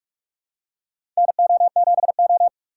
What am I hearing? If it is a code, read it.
Morse code, heard as a keyed tone, about 46 wpm, 700 Hz. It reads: NO7O
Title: USA Call Signs - 4 Characters - 6x Character Spacing - 45wpm